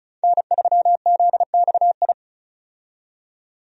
Morse code, heard as a keyed tone, about 34 wpm, 700 Hz. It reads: N3ZXI
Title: USA Call Signs - 5 Characters - 35wpm